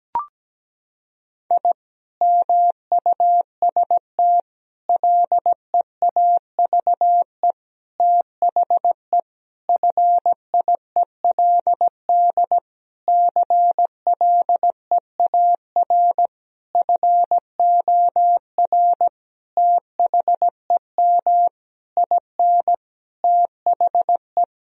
Morse code, heard as a keyed tone, about 17 wpm, 700 Hz. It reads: I MUST LEAVE THE FIELD CLEAR FOR THEM IN THE
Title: Sentences from Top 1000 Words - Review Part 2 - 17wpm